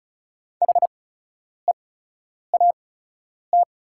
Morse code, heard as a keyed tone, about 34 wpm, spaced as for 10 wpm, 690 Hz. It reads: HEAT
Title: 8x Character Spacing - Top 500 Words - 35wpm